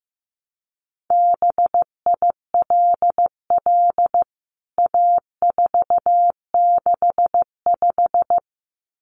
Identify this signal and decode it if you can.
Morse code, heard as a keyed tone, about 15 wpm, 700 Hz. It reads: BILL A465
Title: CWT Exchange 15wpm